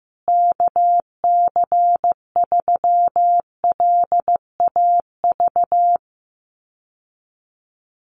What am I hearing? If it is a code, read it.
Morse code, heard as a keyed tone, about 15 wpm, 700 Hz. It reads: KC3LAV
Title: USA Call Signs Repeated 3x 15wpm